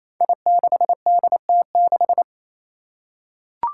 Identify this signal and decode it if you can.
Morse code, heard as a keyed tone, about 28 wpm, 700 Hz. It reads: I6DT6
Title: Five Challenging Letter-Number Combinations 28wpm